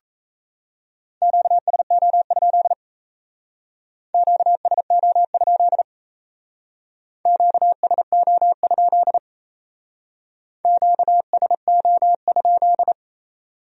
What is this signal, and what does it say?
Morse code, heard as a keyed tone, about 34 wpm, 700 Hz. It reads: QSO? QSO? QSO? QSO?
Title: Speed-Racing Q-Codes - 28wpm